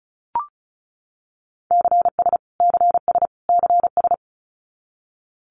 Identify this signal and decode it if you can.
Morse code, heard as a keyed tone, about 34 wpm, 700 Hz. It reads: CS CS CS
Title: Abbreviations Repeated 3x - 35wpm